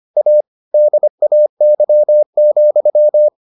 Morse code, heard as a keyed tone, about 25 wpm, 600 Hz. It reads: A DAY,